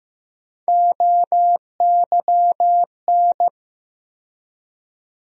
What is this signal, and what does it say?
Morse code, heard as a keyed tone, about 15 wpm, 700 Hz. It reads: OYN
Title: USA Call Sign Postfixes - 3 Letters 15wpm